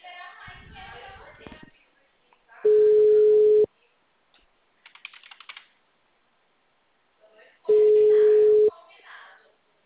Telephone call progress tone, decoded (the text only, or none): ringback tone